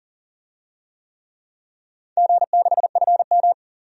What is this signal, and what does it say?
Morse code, heard as a keyed tone, about 40 wpm, 700 Hz. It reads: G6FM
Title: DXCC-100 Call Signs - 4 Characters - 40wpm